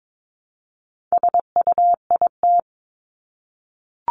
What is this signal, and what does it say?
Morse code, heard as a keyed tone, about 22 wpm, 700 Hz. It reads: SUIT
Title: Top 1501-1750 Words 22wpm